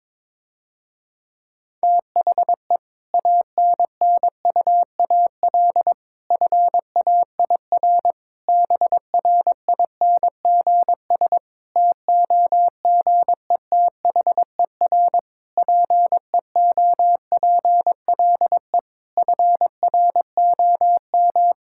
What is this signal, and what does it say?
Morse code, heard as a keyed tone, about 22 wpm, 700 Hz. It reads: THE ANNUAL FAIR BRINGS TOGETHER PEOPLE FROM
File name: Difficult Sentences with Hints - 22wpm